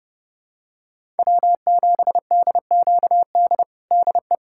Morse code, heard as a keyed tone, about 30 wpm, 700 Hz. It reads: W7DQD DE
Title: USA Callsign DE Callsign - 5 Characters - 30wpm